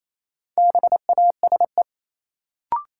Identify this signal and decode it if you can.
Morse code, heard as a keyed tone, about 28 wpm, 700 Hz. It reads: BASE
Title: Sentences from Top 1600 Words - New Word Drill Alt - 28wpm